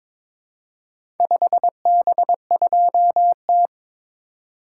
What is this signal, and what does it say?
Morse code, heard as a keyed tone, about 22 wpm, 700 Hz. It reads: 5B2T
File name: practice / 22wpm / Four Letter-Number Combinations - 22wpm